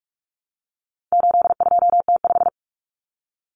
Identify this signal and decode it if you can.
Morse code, heard as a keyed tone, about 44 wpm, 700 Hz. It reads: 82T5
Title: Common Cut Numbers - 4-Digit 45wpm